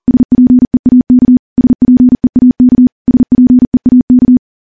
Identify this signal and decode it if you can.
Morse code, heard as a keyed tone, about 40 wpm, 250 Hz. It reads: SPEAK SPEAK SPEAK
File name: morse / morse-training-40WPM